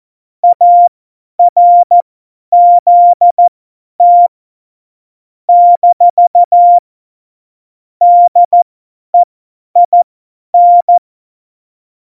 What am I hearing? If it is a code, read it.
Morse code, heard as a keyed tone, about 14 wpm, spaced as for 10 wpm, 700 Hz. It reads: ARZT - DEIN
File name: Morsekurs / Witze / Witz 12. Dezember 2017, 50 F70